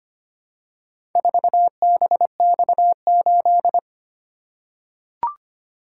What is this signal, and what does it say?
Morse code, heard as a keyed tone, about 25 wpm, 700 Hz. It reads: 4BX8
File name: practice / 25wpm / Four Letter-Number Combinations - 25wpm